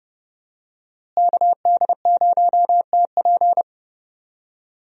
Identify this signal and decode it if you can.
Morse code, heard as a keyed tone, about 32 wpm, 700 Hz. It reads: KD0TP
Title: USA Call Signs - 5 Characters - Repeated 3x - 30wpm